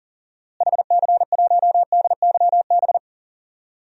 Morse code, heard as a keyed tone, about 40 wpm, 700 Hz. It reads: HC1DYB
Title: DXCC-100 Call Signs - 6 Characters - Repeated 2x - 40wpm